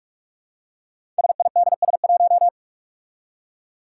Morse code, heard as a keyed tone, about 45 wpm, 700 Hz. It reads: SIDS1
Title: Five Challenging Letter-Number Combinations 45wpm